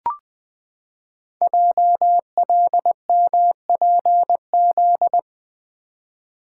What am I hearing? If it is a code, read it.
Morse code, heard as a keyed tone, about 20 wpm, 700 Hz. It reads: JLMPZ
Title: Code Groups - 5 Letters - 20wpm